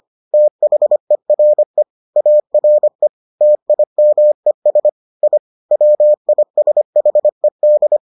Morse code, heard as a keyed tone, about 25 wpm, 600 Hz. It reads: THERE ARE TIMES I WISHED